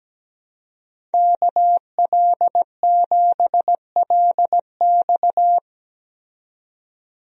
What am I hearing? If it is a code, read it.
Morse code, heard as a keyed tone, about 17 wpm, 700 Hz. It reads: KL7LX
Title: DXCC-100 Call Signs - 5 Characters - 17wpm